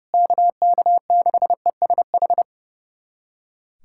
Morse code, heard as a keyed tone, about 32 wpm, 700 Hz. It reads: KK6ESH
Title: USA Callsign DE Callsign 30wpm